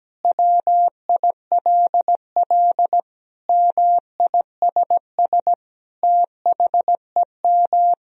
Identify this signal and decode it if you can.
Morse code, heard as a keyed tone, about 17 wpm, 700 Hz. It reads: WILL MISS THEM
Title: Sentences from Top 1000 Words - Review Part 3 - 17wpm